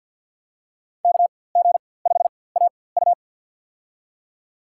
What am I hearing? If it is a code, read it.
Morse code, heard as a keyed tone, about 51 wpm, spaced as for 23 wpm, 700 Hz. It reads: KG5AU
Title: USA Call Signs - 5 Characters - 4x Character Spacing - 50wpm